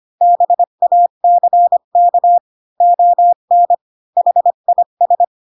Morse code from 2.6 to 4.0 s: ON